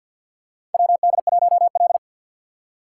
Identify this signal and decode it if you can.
Morse code, heard as a keyed tone, about 53 wpm, 700 Hz. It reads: WD1L
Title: USA Call Signs - 4 Characters - 2x Character Spacing - 50wpm